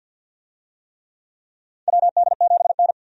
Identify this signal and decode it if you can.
Morse code, heard as a keyed tone, about 52 wpm, 700 Hz. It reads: WD7N
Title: USA Call Signs - 4 Characters - 50wpm